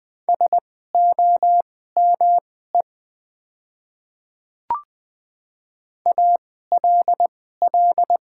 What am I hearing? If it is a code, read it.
Morse code, heard as a keyed tone, about 20 wpm, spaced as for 15 wpm, 700 Hz. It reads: SOME ALL
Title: 2x Character Spacing - Top 100 Words - 20wpm